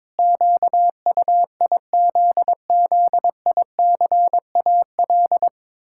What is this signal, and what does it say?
Morse code, heard as a keyed tone, about 22 wpm, 700 Hz. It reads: QUIZZICAL
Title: Pangram Challenge v2 - 22wpm